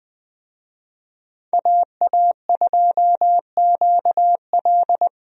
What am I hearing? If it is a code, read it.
Morse code, heard as a keyed tone, about 20 wpm, 700 Hz. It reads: AA2QL
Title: USA Call Signs - 5 Characters - 20wpm